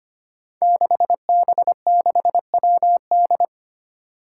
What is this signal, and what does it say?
Morse code, heard as a keyed tone, about 25 wpm, 700 Hz. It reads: 6B6WD
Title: Five Challenging Letter-Number Combinations 25wpm